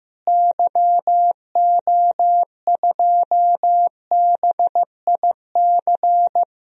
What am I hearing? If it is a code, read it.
Morse code, heard as a keyed tone, about 15 wpm, 700 Hz. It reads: YO2BIC
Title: DXCC-100 Call Signs - 6 Characters - Repeated 3x - 15wpm